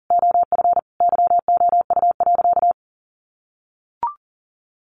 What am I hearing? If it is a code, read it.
Morse code, heard as a keyed tone, about 40 wpm, 700 Hz. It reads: OF YOU.
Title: Sentences from Top 400 Words 40wpm